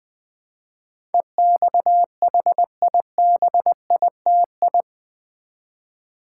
Morse code, heard as a keyed tone, about 20 wpm, 700 Hz. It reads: EXHIBITI
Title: Build A Word - 4 Letters At A Time - 10-Letter Words - 20wpm